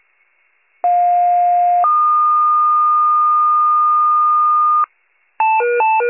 Two-tone page, heard as a one-second tone, then a three-second tone, about 710 then 1220 Hz. It is heard from 0.8 to 4.8 s.